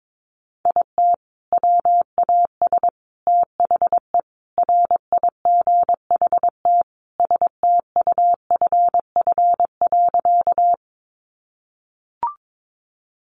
Morse code, heard as a keyed tone, about 22 wpm, 700 Hz. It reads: IT WAS THE RIGHT STUFF.
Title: Sentences from Top 1400 Words 22wpm